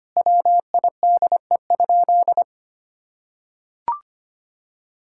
Morse code, heard as a keyed tone, about 25 wpm, 700 Hz. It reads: WIDE?
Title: Sentences from Top 1000 Words - Review Part 6 - 25wpm